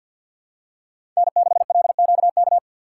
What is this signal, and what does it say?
Morse code, heard as a keyed tone, about 52 wpm, 700 Hz. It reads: N6LQK
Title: USA Call Signs - 5 Characters - 6x Character Spacing - 50wpm